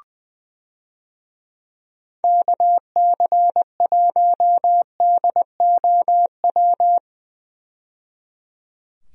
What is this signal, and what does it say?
Morse code, heard as a keyed tone, about 20 wpm, 700 Hz. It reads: KC1DOW
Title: USA Call Signs - 6 Characters - 20wpm